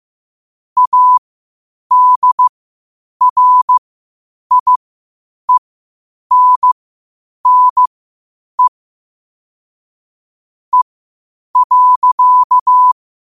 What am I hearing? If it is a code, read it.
Morse code, heard as a keyed tone, about 15 wpm, spaced as for 8 wpm, 1000 Hz. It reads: ADRIENNE E.
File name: qotdfiles / 8wpm